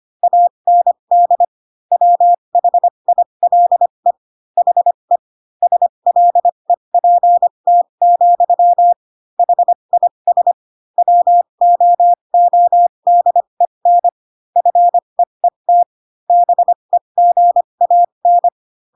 Morse code, heard as a keyed tone, about 25 wpm, spaced as for 22 wpm, 700 Hz. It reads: AND WHILE HE SLEPT, HIS WOODEN FEET BEGAN
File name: Pinocchio / Pinocchio ch6 18 25 ik1hge